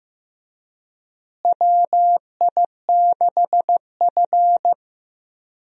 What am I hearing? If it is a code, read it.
Morse code, heard as a keyed tone, about 15 wpm, 700 Hz. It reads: WI6F
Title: USA Call Signs - 4 Characters - Repeated 2x - 15wpm